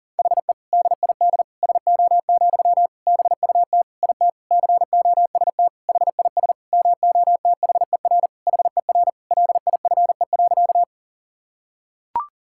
Morse code, heard as a keyed tone, about 37 wpm, 700 Hz. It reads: HE DID SO, BUT IT COST HIS MOTHER HER LIFE.
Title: Sentences from Top 1000 Words - Review Part 5 - 40wpm